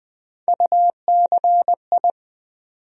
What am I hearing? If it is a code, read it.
Morse code, heard as a keyed tone, about 20 wpm, 700 Hz. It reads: UCI
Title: USA Call Sign Postfixes - 3 Letters 20wpm